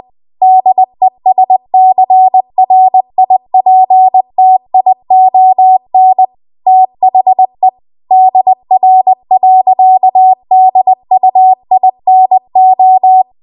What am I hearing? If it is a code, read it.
Morse code, heard as a keyed tone, about 20 wpm, 750 Hz. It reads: DESCRIPTION THE DR.DUINO